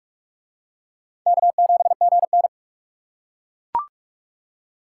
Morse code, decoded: K7GN

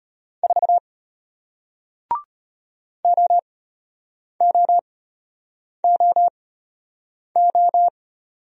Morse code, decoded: 4 OOOO

